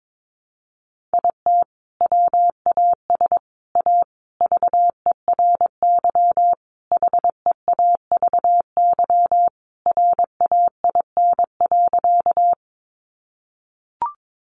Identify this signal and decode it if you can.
Morse code, heard as a keyed tone, about 22 wpm, 700 Hz. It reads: IT WAS A VERY HEAVY RAIN.
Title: Sentences from Top 1100 Words 22wpm